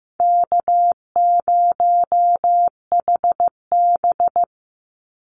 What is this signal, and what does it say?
Morse code, heard as a keyed tone, about 15 wpm, 700 Hz. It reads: K0HB
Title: Call Signs Repeated 2x - 15wpm